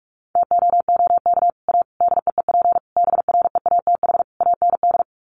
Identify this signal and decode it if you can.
Morse code, heard as a keyed tone, about 44 wpm, 700 Hz. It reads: TOOK A DEEP BREATH AND
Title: Sets of 5 Words Encore 45wpm